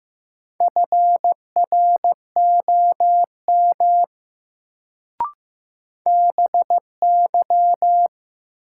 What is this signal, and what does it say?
Morse code, heard as a keyed tone, about 15 wpm, 700 Hz. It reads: FROM BY
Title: Sentences from Top 100 Words - New Word Drill - 15wpm